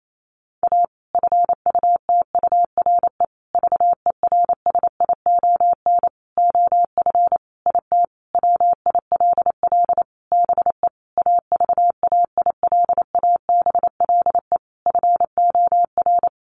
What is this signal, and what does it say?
Morse code, heard as a keyed tone, about 28 wpm, 700 Hz. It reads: A FUTURE VERSION OF IT WILL BE AVAILABLE FOR
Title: Sentences from Top 1400 Words 28wpm